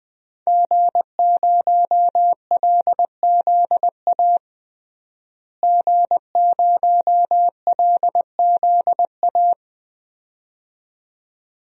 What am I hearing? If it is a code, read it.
Morse code, heard as a keyed tone, about 20 wpm, 700 Hz. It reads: G0LZA G0LZA
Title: DXCC-100 Call Signs - 5 Characters - Repeated 2x - 20wpm